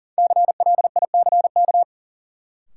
Morse code, heard as a keyed tone, about 40 wpm, 700 Hz. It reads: CLICK
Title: Sentences from Top 1100 Words - New Word Drill - 40wpm